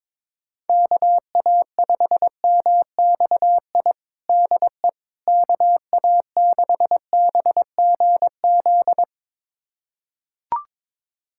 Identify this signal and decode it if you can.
Morse code, heard as a keyed tone, about 22 wpm, 700 Hz. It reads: KA5MXI DE KA6BGZ
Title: USA Callsign DE Callsign - 6 Characters - 22wpm